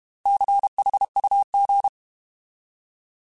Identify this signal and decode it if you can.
Morse code, heard as a keyed tone, about 32 wpm, 800 Hz. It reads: CHUG
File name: Alphabetical / Alphabetical mp3s / Cs @30